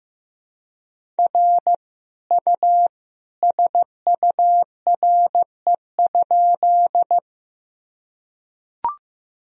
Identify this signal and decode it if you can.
Morse code, heard as a keyed tone, about 16 wpm, 700 Hz. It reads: R U SURE?
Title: Sentences from Top 300 Words and Abbreviations 15wpm